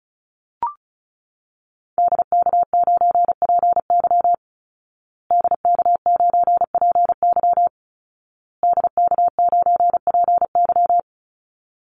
Morse code, heard as a keyed tone, about 34 wpm, 700 Hz. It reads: DK9PY DK9PY DK9PY
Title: Call Signs Repeated 2x - 35wpm